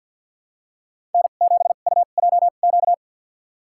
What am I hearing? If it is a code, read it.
Morse code, heard as a keyed tone, about 51 wpm, spaced as for 36 wpm, 700 Hz. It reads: N7UJQ
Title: USA Call Signs - 2x Character Spacing - 50wpm